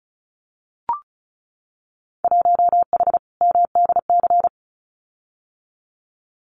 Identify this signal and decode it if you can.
Morse code, heard as a keyed tone, about 36 wpm, 700 Hz. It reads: 1H MDC
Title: Winter Field Day Exchanges 35wpm